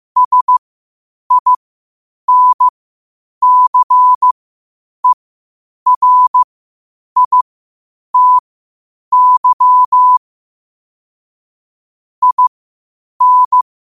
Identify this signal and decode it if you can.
Morse code, heard as a keyed tone, about 15 wpm, spaced as for 8 wpm, 1000 Hz. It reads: SINCERITY IN
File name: qotdfiles / 8wpm